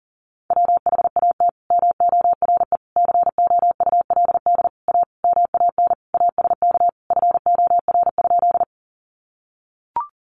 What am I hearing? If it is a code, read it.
Morse code, heard as a keyed tone, about 40 wpm, 700 Hz. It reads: WHAT MORE COULD A MAN ASK FOR?
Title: Sentences from Top 200 Words 40wpm